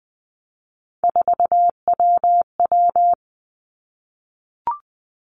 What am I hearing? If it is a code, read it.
Morse code, heard as a keyed tone, about 20 wpm, 700 Hz. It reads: 4WW
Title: Three Challenging Letter-Number Combinations 20wpm